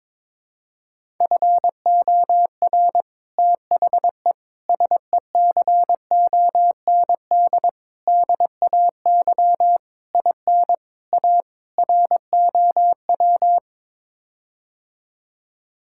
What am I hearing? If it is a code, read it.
Morse code, heard as a keyed tone, about 22 wpm, 700 Hz. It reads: FOR THE SECOND DAY IN A ROW
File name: practice / 22wpm / Sets of 7 Words - 22wpm